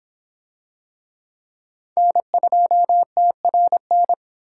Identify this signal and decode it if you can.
Morse code, heard as a keyed tone, about 24 wpm, 700 Hz. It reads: N2TRN